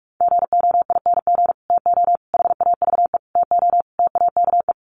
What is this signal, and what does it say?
Morse code, heard as a keyed tone, about 44 wpm, 700 Hz. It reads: GOING TO HAVE TO TAKE
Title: Sets of 5 Words Encore 45wpm